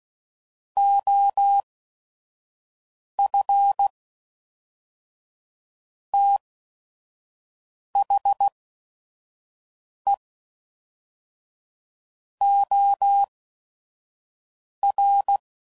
Morse code, heard as a keyed tone, about 16 wpm, spaced as for 5 wpm, 800 Hz. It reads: OF THE OR
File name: CW Code Groups @5WPM